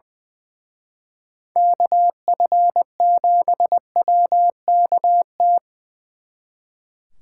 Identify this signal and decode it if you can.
Morse code, heard as a keyed tone, about 24 wpm, 700 Hz. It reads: KF7WKT
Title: Speed Racing - USA Call Signs 20wpm